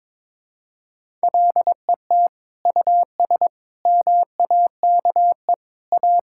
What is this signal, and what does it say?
Morse code, heard as a keyed tone, about 22 wpm, 700 Hz. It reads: LET US MAKE A
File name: practice / 22wpm / Sentences with 2-Word Phrases - 22wpm